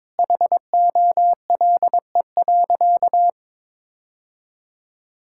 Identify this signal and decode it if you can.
Morse code, heard as a keyed tone, about 22 wpm, 700 Hz. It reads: HOLE.
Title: Sentences from Top 1600 Words 22wpm